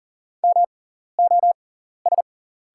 Morse code, heard as a keyed tone, about 40 wpm, spaced as for 14 wpm, 690 Hz. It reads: MOS
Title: 6x Character Spacing - Top 200 Words - 40wpm